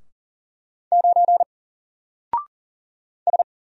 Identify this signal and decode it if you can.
Morse code, heard as a keyed tone, about 39 wpm, 700 Hz. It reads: 9 S
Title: Single Letter-Number Alt 40wpm